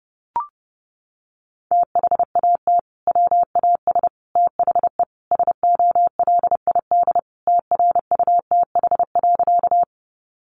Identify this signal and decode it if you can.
Morse code, heard as a keyed tone, about 28 wpm, 700 Hz. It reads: THAT WAS THE SOLID TRUTH.
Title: Sentences from Top 1500 Words 30wpm